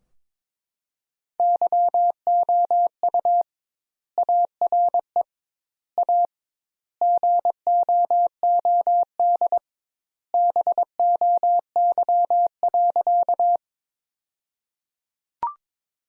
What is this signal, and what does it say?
Morse code, heard as a keyed tone, about 22 wpm, 700 Hz. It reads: YOU ARE A GOOD BOY.
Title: Sentences Top 200 Words - 22wpm